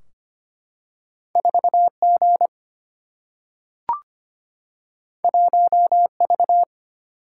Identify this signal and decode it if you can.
Morse code, heard as a keyed tone, about 25 wpm, 700 Hz. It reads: 4G 1V
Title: Single-Letter Single-Number 25wpm